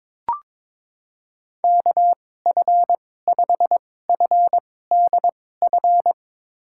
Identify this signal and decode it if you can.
Morse code, heard as a keyed tone, about 22 wpm, spaced as for 16 wpm, 700 Hz. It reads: KF5FDF
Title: USA Call Signs - 6 Characters - 2x Character Spacing - 22wpm